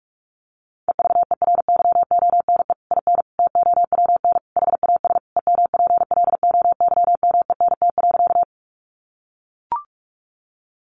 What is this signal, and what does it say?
Morse code, heard as a keyed tone, about 44 wpm, 700 Hz. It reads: EVERYONE IN TOWN HAS EMPLOYMENT.
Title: Sentences from Top 1500 Words 45wpm